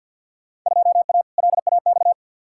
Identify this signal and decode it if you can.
Morse code, heard as a keyed tone, about 49 wpm, 700 Hz. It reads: 2A LAX
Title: Field Day Exchanges - 50wpm